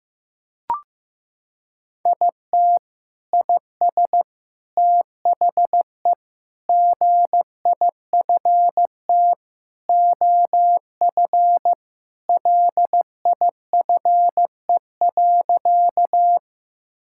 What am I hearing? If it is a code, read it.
Morse code, heard as a keyed tone, about 15 wpm, 700 Hz. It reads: IT IS THE GIFT OF LIFE.
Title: Sentences from Top 1000 Words - Review Part 6 - 15wpm